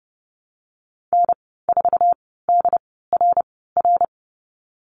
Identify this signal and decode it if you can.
Morse code, heard as a keyed tone, about 30 wpm, spaced as for 17 wpm, 700 Hz. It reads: N4DRR
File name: Lesson 15g - Number 5 - Call Signs 30wpm